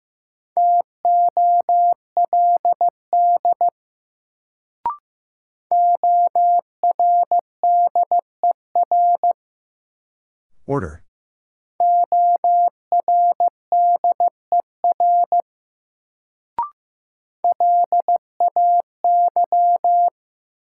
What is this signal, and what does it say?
Morse code, heard as a keyed tone, about 15 wpm, 700 Hz. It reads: TOLD ORDER ORDER LAY
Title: Sentences from Top 400 Words - New Word Drill - 15wpm